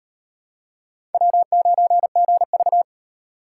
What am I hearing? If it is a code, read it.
Morse code, heard as a keyed tone, about 38 wpm, 700 Hz. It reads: W9GV